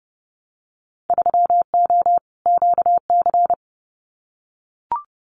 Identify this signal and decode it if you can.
Morse code, heard as a keyed tone, about 32 wpm, 700 Hz. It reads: 3O QC
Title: Winter Field Day Exchanges Encore 30wpm